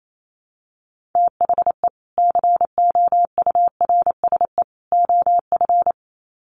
Morse code, heard as a keyed tone, about 28 wpm, 700 Hz. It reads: THE COURSE OF
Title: Sets of 3 Words 28wpm